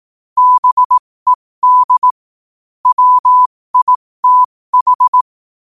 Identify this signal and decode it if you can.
Morse code, heard as a keyed tone, about 18 wpm, spaced as for 15 wpm, 1000 Hz. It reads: BED WITH
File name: qotdfiles / 15wpm